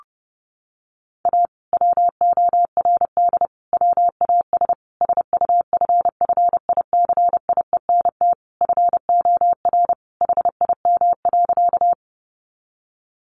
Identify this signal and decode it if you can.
Morse code, heard as a keyed tone, about 30 wpm, 700 Hz. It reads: A WORD WAS SUFFICIENT FOR HIM.
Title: Sentences from Top 1400 Words 30wpm